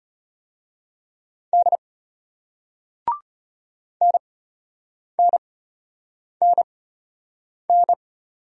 Morse code, heard as a keyed tone, about 33 wpm, spaced as for 8 wpm, 700 Hz. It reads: D NNNN